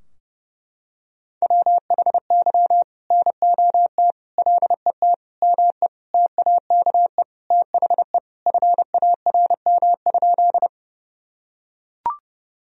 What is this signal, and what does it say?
Morse code, heard as a keyed tone, about 32 wpm, 700 Hz. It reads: WHY NOT LET ME TAKE THE FARM?
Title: Sentences from Top 1000 Words - Review Part 2 - 30wpm